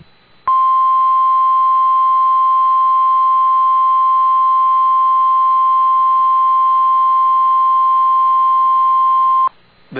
Emergency Alert System (SAME) tones (the 1050 Hz attention tone). It is heard from 0.4 to 9.5 s.